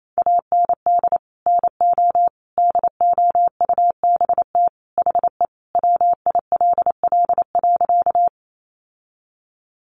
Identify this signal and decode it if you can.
Morse code, heard as a keyed tone, about 28 wpm, 700 Hz. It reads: AND NO DOUBT HE WILL.
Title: Sentences from Top 1500 Words 28wpm